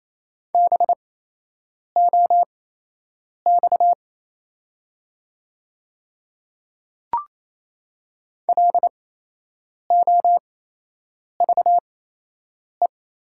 Morse code, heard as a keyed tone, about 28 wpm, spaced as for 8 wpm, 700 Hz. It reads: BOX LOVE
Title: 8x Character Spacing - Top 500 Words - 28wpm